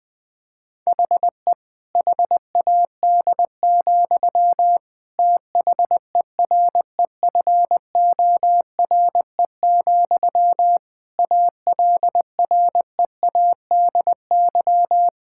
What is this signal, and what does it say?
Morse code, heard as a keyed tone, about 20 wpm, 700 Hz. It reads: HE HAD, THEREFORE, ALREADY
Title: Sentences from Top 1000 Words - Review Part 5 - 20wpm